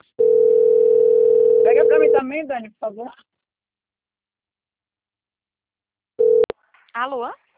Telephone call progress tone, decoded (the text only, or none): ringback tone